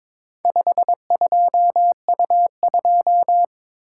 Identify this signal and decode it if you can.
Morse code, heard as a keyed tone, about 22 wpm, 700 Hz. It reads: 52U2
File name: practice / 22wpm / Four Letter-Number Combinations - 22wpm